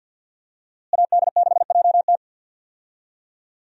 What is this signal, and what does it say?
Morse code, heard as a keyed tone, about 51 wpm, 700 Hz. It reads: AD6JT
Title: USA Call Signs - 5 Characters - Repeated 2x - 50wpm